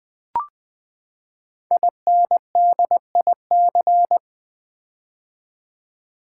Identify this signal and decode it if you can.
Morse code, heard as a keyed tone, about 20 wpm, 700 Hz. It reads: INDIC
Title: Build A Word - 5 Letters At A Time - 10-Letter Words - 20wpm